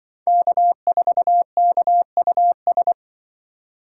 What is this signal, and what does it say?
Morse code, heard as a keyed tone, about 24 wpm, 700 Hz. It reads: K4KUS
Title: Speed Racing - USA 1x3 Call Signs 20wpm